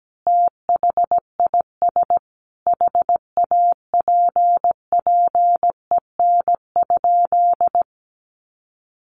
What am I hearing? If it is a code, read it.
Morse code, heard as a keyed tone, about 17 wpm, 700 Hz. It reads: THIS HAPPEN?
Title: Sentences from Top 400 Words 17wpm